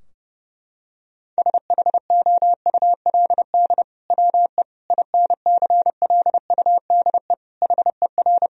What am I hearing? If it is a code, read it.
Morse code, heard as a keyed tone, about 32 wpm, 700 Hz. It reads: SHOULD WE INCLUDE HER
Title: Sentences from Top 1000 Words - Review Part 3 - 30wpm